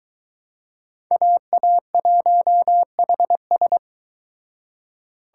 Morse code, heard as a keyed tone, about 24 wpm, 700 Hz. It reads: AA1HS